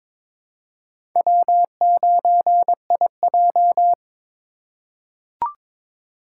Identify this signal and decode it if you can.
Morse code, heard as a keyed tone, about 22 wpm, 700 Hz. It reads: W9IJ
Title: USA Call Signs - 4 Characters - 2x Character Spacing - 22wpm